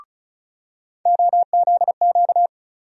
Morse code, heard as a keyed tone, about 36 wpm, 700 Hz. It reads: OZQ
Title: USA Call Sign Postfixes - 3 Letters 35wpm